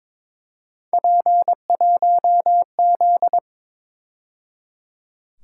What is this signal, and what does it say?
Morse code, heard as a keyed tone, about 26 wpm, 700 Hz. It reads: P1Z
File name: Speed-Racing - Three Letter-Number Combinations - 22wpm